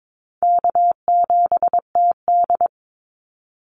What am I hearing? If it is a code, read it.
Morse code, heard as a keyed tone, about 22 wpm, 700 Hz. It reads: K7TD
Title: Call Signs - 22wpm